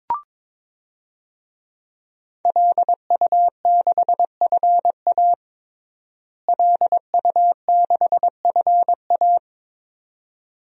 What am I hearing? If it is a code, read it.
Morse code, heard as a keyed tone, about 22 wpm, 700 Hz. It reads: LU6FA LU6FA
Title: DXCC-100 Call Signs - 5 Characters - Repeated 2x - 22wpm